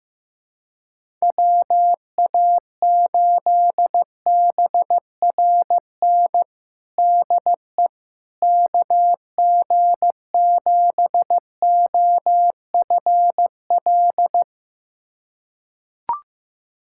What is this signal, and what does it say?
Morse code, heard as a keyed tone, about 15 wpm, 700 Hz. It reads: WA8BRN DE KG7OFL
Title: USA Callsign DE Callsign - 6 Characters - 15wpm